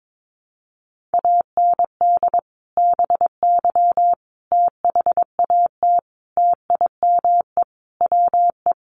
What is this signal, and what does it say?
Morse code, heard as a keyed tone, about 22 wpm, 700 Hz. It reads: AND BY THAT TIME WE
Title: Sentences from Top 1100 Words 22wpm